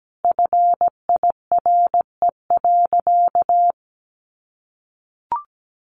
Morse code, heard as a keyed tone, about 17 wpm, 700 Hz. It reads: FIRE.
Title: Sentences from Top 1500 Words 17wpm